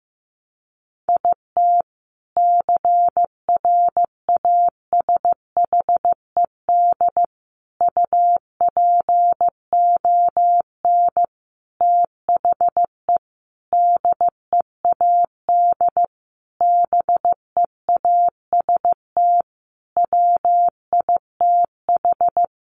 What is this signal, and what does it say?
Morse code, heard as a keyed tone, about 15 wpm, 700 Hz. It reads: IT CRASHED UPON THE DEAD BEAST WITH
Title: A Sound Of Thunder 15wpm